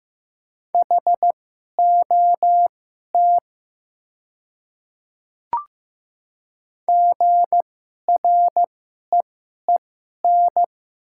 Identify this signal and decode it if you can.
Morse code, heard as a keyed tone, about 15 wpm, spaced as for 11 wpm, 700 Hz. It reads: HOT GREEN
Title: 2x Character Spacing - Top 500 Words - 15wpm